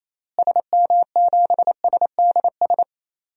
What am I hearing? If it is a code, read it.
Morse code, heard as a keyed tone, about 28 wpm, 700 Hz. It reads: SM7SDS
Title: DXCC-100 Call Signs - 6 Characters - Repeated 3x - 28wpm